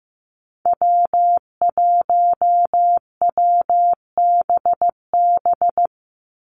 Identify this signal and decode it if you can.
Morse code, heard as a keyed tone, about 15 wpm, 700 Hz. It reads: W1WBB
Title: Call Signs Repeated 2x - 15wpm